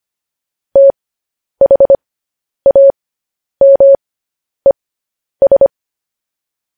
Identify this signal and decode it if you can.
Morse code, heard as a keyed tone, about 25 wpm, spaced as for 10 wpm, 550 Hz. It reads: THAMES
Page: news-headlines-in-morse-code-at-10-wpm